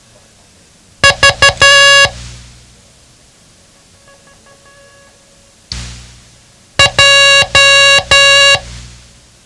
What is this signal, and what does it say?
Morse code, heard as a keyed tone, about 12 wpm, 2800 Hz. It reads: V J
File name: ham radio / CodeQuick / Code Quick 07